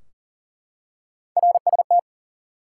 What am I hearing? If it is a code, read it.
Morse code, heard as a keyed tone, about 40 wpm, 700 Hz. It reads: RST
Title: Top 90 QSO Elements 40wpm